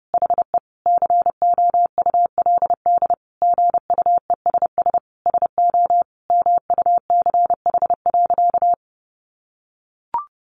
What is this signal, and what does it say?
Morse code, heard as a keyed tone, about 30 wpm, 700 Hz. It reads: HE COULD GUESS SO MUCH.
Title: Sentences from Top 1500 Words 30wpm